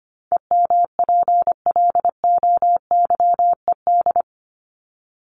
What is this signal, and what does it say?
Morse code, heard as a keyed tone, about 25 wpm, 700 Hz. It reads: EMPLOYED
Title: ED Suffix - Top 100 Words with Base Words 25wpm